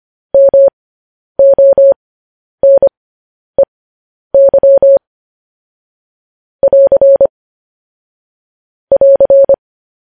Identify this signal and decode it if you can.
Morse code, heard as a keyed tone, about 25 wpm, spaced as for 10 wpm, 550 Hz. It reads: MONEY <AR> <AR>